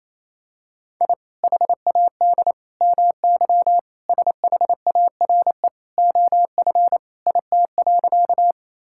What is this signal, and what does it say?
Morse code, heard as a keyed tone, about 27 wpm, 700 Hz. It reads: I HAD MY SHARE OF IT.
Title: Sentences from Top 1000 Words - Review Part 6 - 28wpm